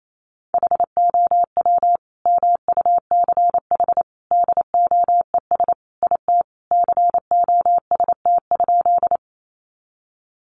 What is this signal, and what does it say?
Morse code, heard as a keyed tone, about 28 wpm, 700 Hz. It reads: HOW MUCH DOES IT COST?
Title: Sentences from Top 800 Words 28wpm